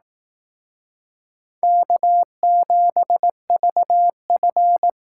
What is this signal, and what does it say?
Morse code, heard as a keyed tone, about 18 wpm, 700 Hz. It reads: K7VF
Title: Speed Racing - USA 1x2 Call Signs 15wpm